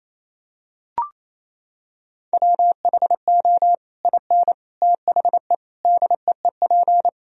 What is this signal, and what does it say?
Morse code, heard as a keyed tone, about 28 wpm, 700 Hz. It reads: WHO IN THE DEEP
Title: Sentences from Top 1000 Words - Review Part 3 - 28wpm